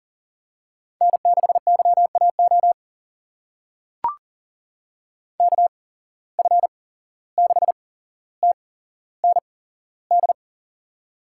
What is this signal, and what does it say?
Morse code, heard as a keyed tone, about 40 wpm, spaced as for 16 wpm, 700 Hz. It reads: N6YAO KF6TND
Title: USA Call Signs - 8x Character Spacing - 40wpm